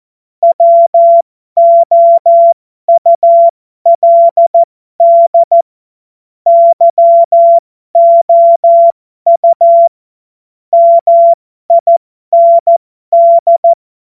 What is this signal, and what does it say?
Morse code, heard as a keyed tone, about 14 wpm, spaced as for 12 wpm, 680 Hz. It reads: WOULD YOU MIND